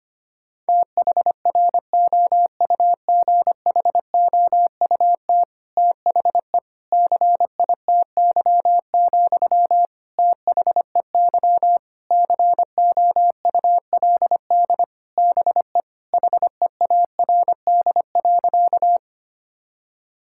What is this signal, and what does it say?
Morse code, heard as a keyed tone, about 25 wpm, 700 Hz. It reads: THROUGHOUT THE CITY, THEY COULD BE HEARD.
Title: Sentences from Top 1000 Words - Review Part 5 - 25wpm